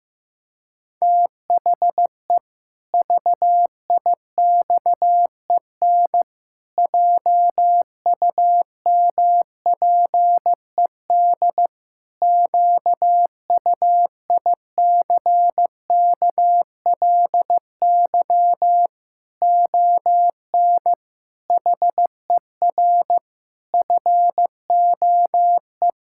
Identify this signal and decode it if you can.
Morse code, heard as a keyed tone, about 15 wpm, 700 Hz. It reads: THE VIXEN JUMPED QUICKLY ON HER FOE
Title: Pangram Challenge v2 - 15wpm